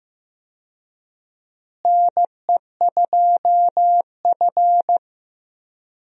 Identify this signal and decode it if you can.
Morse code, heard as a keyed tone, about 15 wpm, 700 Hz. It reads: NE2F
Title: USA Call Signs - 4 Characters - Repeated 3x - 15wpm